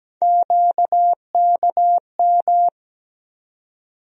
Morse code, heard as a keyed tone, about 17 wpm, 700 Hz. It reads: QKM
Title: USA Call Sign Postfixes - 3 Letters 17wpm